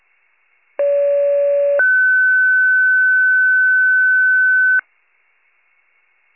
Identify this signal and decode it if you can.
Two-tone page, heard as a one-second tone, then a three-second tone, about 570 then 1500 Hz. It is heard from 0.8 to 4.8 s.
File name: pager / 73 RESCUE / wcalert